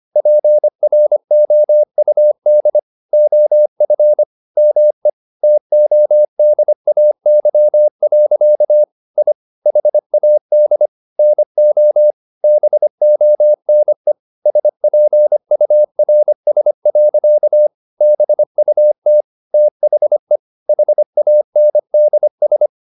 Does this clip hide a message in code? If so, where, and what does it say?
Yes: Morse code from 0.2 to 22.7 s, heard as a keyed tone, about 25 wpm, 600 Hz. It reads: PROUD OF ME TODAY. I HAD NO BONE SPURS. BUT THE HANDS